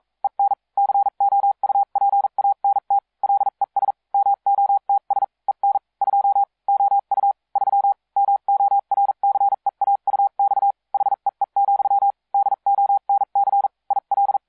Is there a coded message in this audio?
Yes: Morse code from 0.2 to 14.4 s, heard as a keyed tone, about 42 wpm, spaced as for 36 wpm, 800 Hz. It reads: EN COUPANT LES MOTS EN 2 OU 3 MORCEAUX HEE, DONC IL